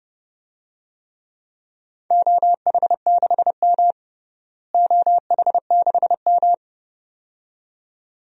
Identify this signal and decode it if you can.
Morse code, heard as a keyed tone, about 30 wpm, 700 Hz. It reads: OH6M OH6M
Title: DXCC-100 Call Signs - 4 Characters - Repeated 2x - 30wpm